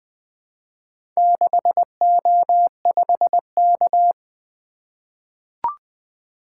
Morse code, heard as a keyed tone, about 20 wpm, 700 Hz. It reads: 6O5K
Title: Four Letter-Number Combinations - 20wpm